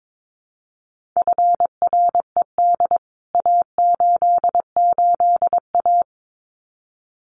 Morse code, heard as a keyed tone, about 22 wpm, 700 Hz. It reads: FRED A88A
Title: CWT Exchange 22wpm